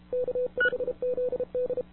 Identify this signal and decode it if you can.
Morse code, heard as a keyed tone, about 33 wpm, 500 Hz. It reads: K5ZD
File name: wwcw08 / k5zd wwcw08 1700z